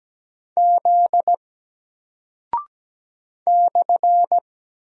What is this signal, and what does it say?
Morse code, heard as a keyed tone, about 17 wpm, 700 Hz. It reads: Z /
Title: Single Character (Letter:Number:Punctuation:Prosign) Alt 17wpm